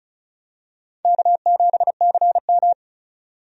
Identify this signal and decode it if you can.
Morse code, heard as a keyed tone, about 34 wpm, 700 Hz. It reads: K7CM
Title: USA Call Signs - 4 Characters - 35wpm